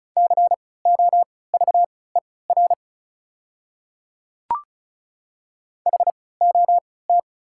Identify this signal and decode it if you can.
Morse code, heard as a keyed tone, about 34 wpm, spaced as for 20 wpm, 700 Hz. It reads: COVER HOT
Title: 3x Character Spacing - Top 300 Words - 35wpm